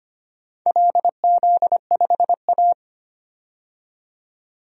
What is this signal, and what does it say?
Morse code, heard as a keyed tone, about 25 wpm, 700 Hz. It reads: LZ5A
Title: DXCC-100 Call Signs - 4 Characters - 25wpm